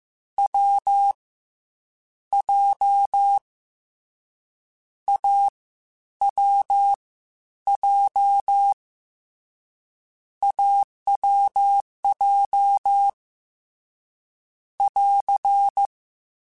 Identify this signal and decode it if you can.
Morse code, heard as a keyed tone, about 15 wpm, 800 Hz. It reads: W J A W J AWJ <AR>
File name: Difficult Combos / Difficult Combos-mp3s / AWJ ShortRun @13